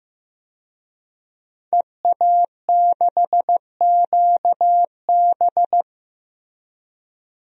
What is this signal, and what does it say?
Morse code, heard as a keyed tone, about 15 wpm, 700 Hz. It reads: EA6QB
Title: DXCC-100 Call Signs - 5 Characters - Repeated 2x - 15wpm